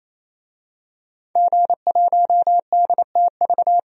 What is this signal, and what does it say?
Morse code, heard as a keyed tone, about 28 wpm, 700 Hz. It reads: G1DTV
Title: DXCC-100 Call Signs - 5 Characters - 28wpm